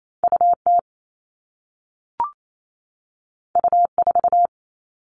Morse code, heard as a keyed tone, about 28 wpm, 700 Hz. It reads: UT U4
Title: All Cut Numbers - 2-Digit 28wpm